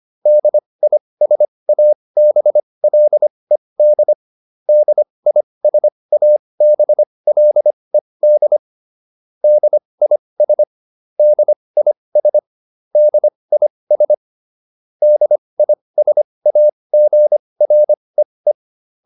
Morse code, heard as a keyed tone, about 25 wpm, spaced as for 20 wpm, 600 Hz. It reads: DISABLED DISABLED DIS DIS DIS DISAGREE